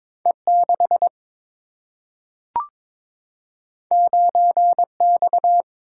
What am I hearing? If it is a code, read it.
Morse code, heard as a keyed tone, about 22 wpm, 700 Hz. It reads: E 6 9 X
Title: Single-Letter Single-Number 22wpm